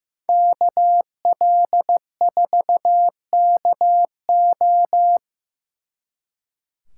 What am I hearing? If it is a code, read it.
Morse code, heard as a keyed tone, about 18 wpm, 700 Hz. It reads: KL4KO
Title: Speed Racing - USA 2x2 Call Signs 15wpm